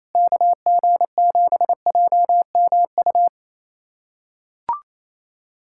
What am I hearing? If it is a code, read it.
Morse code, heard as a keyed tone, about 27 wpm, 700 Hz. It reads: KG7JMU
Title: USA Callsign DE Callsign - 6 Characters - 28wpm